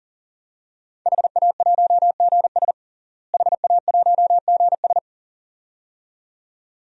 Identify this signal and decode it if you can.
Morse code, heard as a keyed tone, about 40 wpm, 700 Hz. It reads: HA1GS HA1GS
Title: DXCC-100 Call Signs - 5 Characters - Repeated 2x - 40wpm